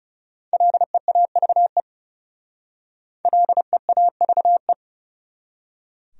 Morse code, heard as a keyed tone, about 40 wpm, 700 Hz. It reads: LEAVE LEAVE